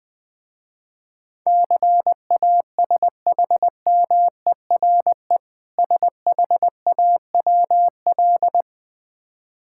Morse code, heard as a keyed tone, about 20 wpm, 700 Hz. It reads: CASHMERE SHAWL
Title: Difficult Words with Hints - 20wpm